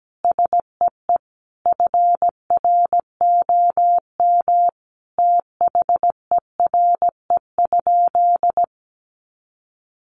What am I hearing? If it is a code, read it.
Morse code, heard as a keyed tone, about 17 wpm, 700 Hz. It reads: SEE FROM THERE?
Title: Sentences from Top 100 Words 17wpm